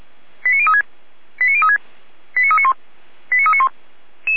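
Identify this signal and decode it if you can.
Five-tone selective call (ZVEI), heard as quick runs of five tones, tones of about 80 ms. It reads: 89036 89036 89391 89391
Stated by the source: ZVEI five-tone selective call received on an SDR.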